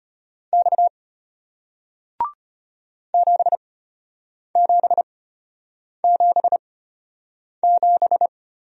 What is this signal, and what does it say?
Morse code, heard as a keyed tone, about 34 wpm, spaced as for 8 wpm, 700 Hz. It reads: X 7777